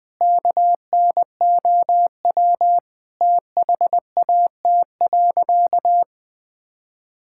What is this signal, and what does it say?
Morse code, heard as a keyed tone, about 20 wpm, 700 Hz. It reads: KNOW THAT.
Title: Sentences with 4-Word Phrases - 20wpm